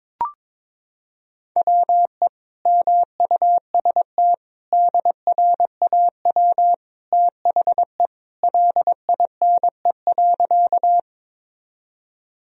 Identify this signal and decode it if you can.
Morse code, heard as a keyed tone, about 22 wpm, 700 Hz. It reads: WE MUST DRAW THE LINE.
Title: Sentences from Top 1000 Words - Review Part 1 - 22wpm